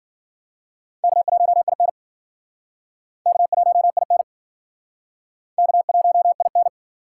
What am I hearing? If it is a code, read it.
Morse code, heard as a keyed tone, about 52 wpm, 700 Hz. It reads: K1IN K1IN K1IN